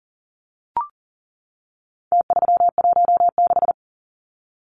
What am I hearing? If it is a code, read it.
Morse code, heard as a keyed tone, about 40 wpm, 700 Hz. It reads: T316
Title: Numbers - Contest Style - 4-Digit 40wpm